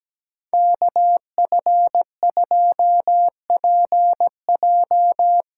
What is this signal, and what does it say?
Morse code, heard as a keyed tone, about 17 wpm, 700 Hz. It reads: KF2PJ
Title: USA Callsign DE Callsign - 5 Characters - 17wpm